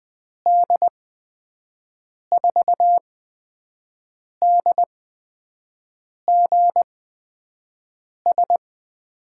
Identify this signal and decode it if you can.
Morse code, heard as a keyed tone, about 20 wpm, spaced as for 5 wpm, 700 Hz. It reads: D4DGS
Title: USA Call Signs - 6 Characters - 8x Character Spacing - 20wpm